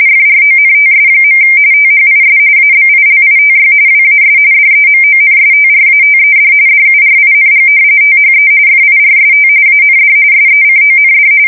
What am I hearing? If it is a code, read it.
RTTY, about 45.45 baud. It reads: FRONTS. THIS ABILITY WAS DEMONSTRATED DURING THE FIRST AND SECOND
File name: fsk170 / usnavy1